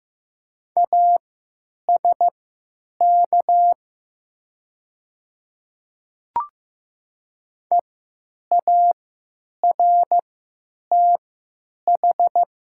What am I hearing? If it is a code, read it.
Morse code, heard as a keyed tone, about 15 wpm, spaced as for 9 wpm, 700 Hz. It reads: ASK EARTH
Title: 3x Character Spacing - Top 200 Words - 15wpm